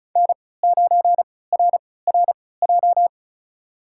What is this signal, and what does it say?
Morse code, heard as a keyed tone, about 36 wpm, spaced as for 20 wpm, 700 Hz. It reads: N9RRJ
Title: USA Call Signs - 3x Character Spacing - 35wpm